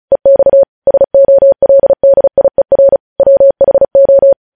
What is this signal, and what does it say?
Morse code, heard as a keyed tone, about 36 wpm, 550 Hz. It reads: EX SOLDIER WHO
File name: podcast / News headlines-1x-35x35